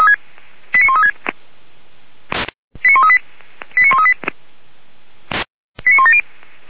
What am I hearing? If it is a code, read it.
Five-tone selective call (ZVEI), heard as quick runs of five tones, tones of about 70 ms. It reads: 89136 89138 89138 89170